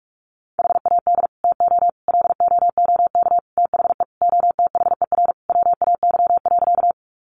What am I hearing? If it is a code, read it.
Morse code, heard as a keyed tone, about 44 wpm, 700 Hz. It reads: HAD TO LOOK THE OTHER WAY.
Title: Sentences from Top 100 Words 45wpm